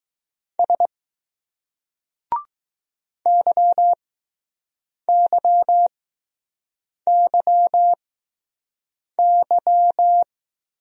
Morse code, heard as a keyed tone, about 20 wpm, spaced as for 6 wpm, 700 Hz. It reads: S YYYY